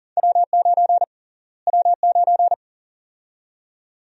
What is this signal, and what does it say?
Morse code, heard as a keyed tone, about 40 wpm, 700 Hz. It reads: W9 W9
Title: USA Call Sign Prefixes - Repeated 2x - 40wpm